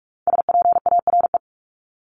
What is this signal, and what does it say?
Morse code, heard as a keyed tone, about 44 wpm, 700 Hz. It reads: SPARE
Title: Top 1751-2000 Words 45wpm